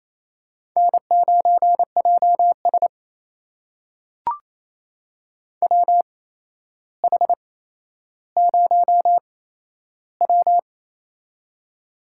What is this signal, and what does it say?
Morse code, heard as a keyed tone, about 28 wpm, spaced as for 12 wpm, 700 Hz. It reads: N9JS WH0W
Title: USA Call Signs - 4 Characters - 8x Character Spacing - 28wpm